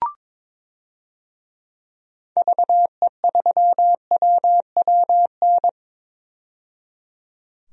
Morse code, heard as a keyed tone, about 22 wpm, 700 Hz. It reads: VE3WWN
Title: DXCC-100 Call Signs - 6 Characters - 22wpm